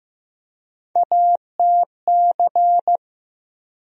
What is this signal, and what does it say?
Morse code, heard as a keyed tone, about 15 wpm, 700 Hz. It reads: ATC
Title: USA Call Sign Postfixes - 3 Letters 15wpm